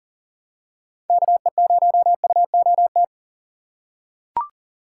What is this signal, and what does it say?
Morse code, heard as a keyed tone, about 37 wpm, 700 Hz. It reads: KE0UOT
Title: USA Call Signs - 6 Characters - 4x Character Spacing - 40wpm